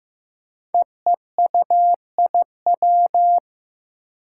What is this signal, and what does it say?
Morse code, heard as a keyed tone, about 15 wpm, 700 Hz. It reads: EEUIW
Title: Five Challenging Letter-Number Combinations 15wpm